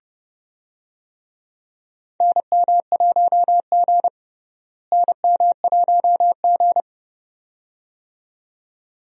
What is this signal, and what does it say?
Morse code, heard as a keyed tone, about 30 wpm, 700 Hz. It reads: NM1G NM1G
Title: USA Call Signs - 4 Characters - Repeated 2x - 30wpm